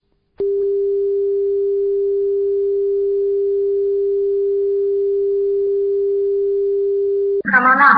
Dial tone (Japan), 400 Hz.